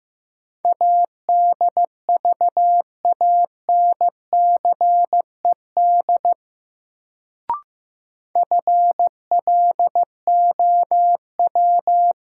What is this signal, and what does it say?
Morse code, heard as a keyed tone, about 15 wpm, 700 Hz. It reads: ADVANCED FLOW
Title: Sentences from Top 1200 Words - New Word Drill Alt - 15wpm